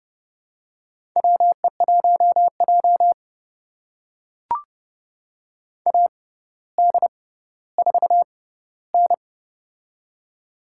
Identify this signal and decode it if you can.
Morse code, heard as a keyed tone, about 30 wpm, spaced as for 15 wpm, 700 Hz. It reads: WE1J AD4N
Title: USA Call Signs - 4 Characters - 6x Character Spacing - 30wpm